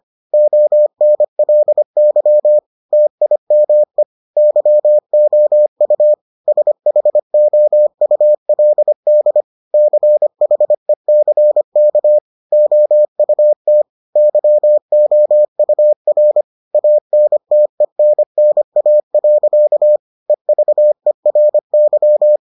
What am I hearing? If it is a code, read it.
Morse code, heard as a keyed tone, about 25 wpm, 600 Hz. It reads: ONLY TIME YOU SHOULD CHECK OUT YOUR ANTENNA. EVERY